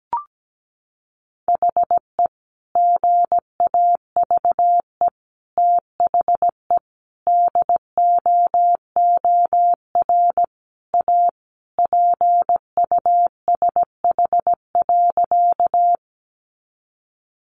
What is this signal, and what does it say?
Morse code, heard as a keyed tone, about 17 wpm, 700 Hz. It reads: HE GAVE THE DOOR A PUSH.
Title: Sentences from Top 1300 Words 17wpm